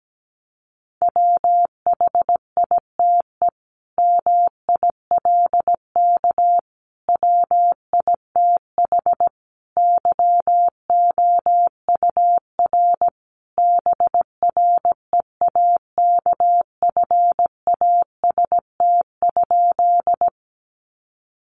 Morse code, heard as a keyed tone, about 17 wpm, 700 Hz. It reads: WHITE MILK WITH YOUR BREAKFAST?
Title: Sentences from Top 1500 Words 17wpm